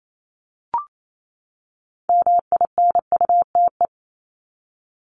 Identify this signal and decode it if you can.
Morse code, heard as a keyed tone, about 28 wpm, 700 Hz. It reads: MINUTE
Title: ES:S Suffix - Top 100 Words with Base Words 28wpm